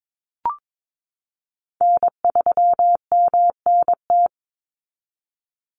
Morse code, heard as a keyed tone, about 22 wpm, 700 Hz. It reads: N3MNT
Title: Call Signs Repeated 3x - 22wpm